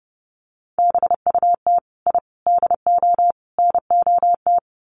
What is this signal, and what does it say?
Morse code, heard as a keyed tone, about 30 wpm, 700 Hz. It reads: BUT I DO NOT
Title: Sets of 4 Words 30wpm